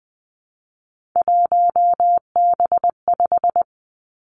Morse code, heard as a keyed tone, about 19 wpm, 700 Hz. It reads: 1B5
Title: Three Challenging Letter-Number Combinations 20wpm